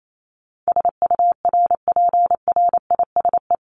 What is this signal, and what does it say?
Morse code, heard as a keyed tone, about 28 wpm, 700 Hz. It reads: SURPRISE
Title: Top 1751-2000 Words 28wpm